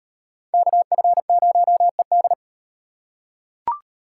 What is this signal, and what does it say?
Morse code, heard as a keyed tone, about 38 wpm, 700 Hz. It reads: KF0ED